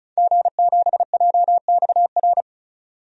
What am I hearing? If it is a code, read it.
Morse code, heard as a keyed tone, about 35 wpm, 700 Hz. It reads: G7JXR
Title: DXCC-100 Call Signs - 5 Characters - 35wpm